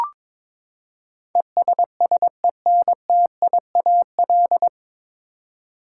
Morse code, heard as a keyed tone, about 21 wpm, 700 Hz. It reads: ESSENTIAL
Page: Morse Code Ninja practice 22wpm